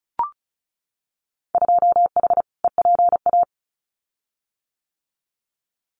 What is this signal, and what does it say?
Morse code, heard as a keyed tone, about 34 wpm, 700 Hz. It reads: 2H EPA
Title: Winter Field Day Exchanges Encore 35wpm